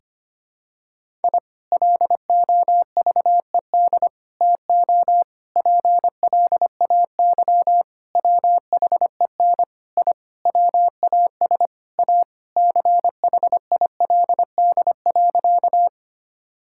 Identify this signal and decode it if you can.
Morse code, heard as a keyed tone, about 25 wpm, 700 Hz. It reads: I LOVED TO PLAY WHEN I WAS A CHILD.
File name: practice / 25wpm / Sentences with 4-Word Phrases - 25wpm